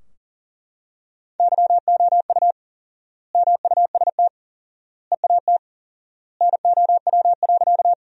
Morse code, heard as a keyed tone, about 40 wpm, 700 Hz. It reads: YOU MUST EAT NOW.
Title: 4x Word-Spacing - Sentences from Top 300 Words - 40wpm